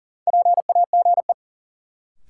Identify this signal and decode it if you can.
Morse code, heard as a keyed tone, about 40 wpm, 700 Hz. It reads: PAGE
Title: Sets of 5 Words Encore - New Word Drill - 40wpm